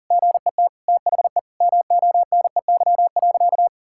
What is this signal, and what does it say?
Morse code, heard as a keyed tone, about 40 wpm, 700 Hz. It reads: GET THE MONEY.
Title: Sentences from Top 1000 Words - Review Part 5 - 40wpm